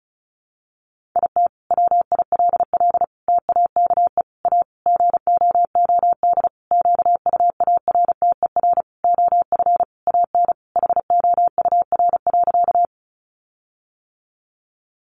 Morse code, heard as a keyed tone, about 34 wpm, 700 Hz. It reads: IT WILL TAKE A GOOD QUARTER OF AN HOUR.
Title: Sentences from Top 1300 Words 35wpm